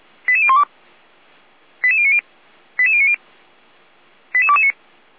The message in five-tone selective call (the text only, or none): 88012 88090 88090 88229